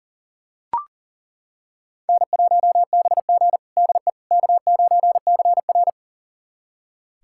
Morse code, heard as a keyed tone, about 40 wpm, 700 Hz. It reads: N1BG DE K9CR
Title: USA Callsign DE Callsign - 4 Characters - 40wpm